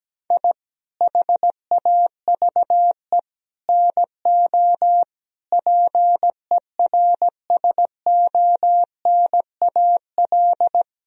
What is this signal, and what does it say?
Morse code, heard as a keyed tone, about 17 wpm, 700 Hz. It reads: I HAVE NO PERSONAL
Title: Sentences from Top 1000 Words - Review Part 4 - 17wpm